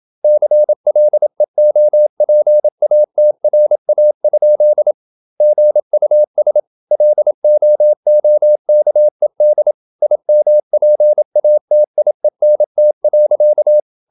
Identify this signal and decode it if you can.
Morse code, heard as a keyed tone, about 28 wpm, 600 Hz. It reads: CLEOPATRA? GUS LOOKED IMPATIENT.
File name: Never 0027wpm